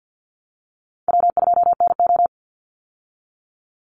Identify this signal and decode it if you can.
Morse code, heard as a keyed tone, about 50 wpm, 700 Hz. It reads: W2NO
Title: Easy Call Signs Repeated 2x - 50wpm